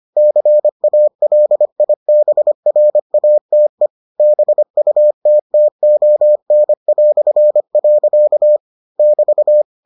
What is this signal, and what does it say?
Morse code, heard as a keyed tone, about 25 wpm, 600 Hz. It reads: CALIBRATE BUTTON". <BT>